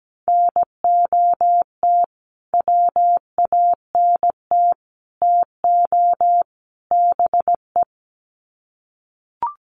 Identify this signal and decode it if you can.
Morse code, heard as a keyed tone, about 17 wpm, 700 Hz. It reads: NOT WANT TO BE
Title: Sets of 4 Words 17wpm